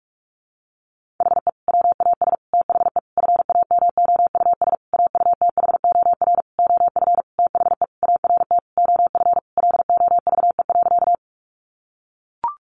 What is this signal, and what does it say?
Morse code, heard as a keyed tone, about 44 wpm, 700 Hz. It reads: HE WAS THE FAMOUS AUTHOR OF THE ART OF LOVE.
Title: Sentences from Top 1500 Words 45wpm